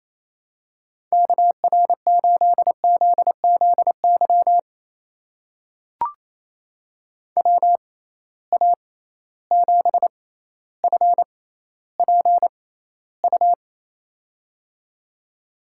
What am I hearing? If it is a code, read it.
Morse code, heard as a keyed tone, about 28 wpm, spaced as for 14 wpm, 700 Hz. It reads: KR8ZZY WA7FPU
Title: USA Call Signs - 6 Characters - 6x Character Spacing - 28wpm